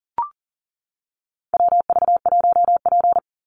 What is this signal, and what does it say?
Morse code, heard as a keyed tone, about 40 wpm, 700 Hz. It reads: WV1P
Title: USA Call Signs 40wpm